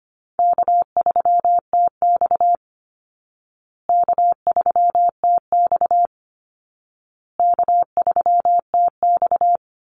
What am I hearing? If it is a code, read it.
Morse code, heard as a keyed tone, about 25 wpm, 700 Hz. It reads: K3TX K3TX K3TX
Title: Easy Call Signs Repeated 3x - 25wpm